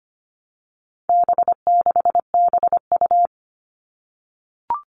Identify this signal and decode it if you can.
Morse code, heard as a keyed tone, about 25 wpm, 700 Hz. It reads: B6BU
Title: All Cut Numbers - 4-Digit 25wpm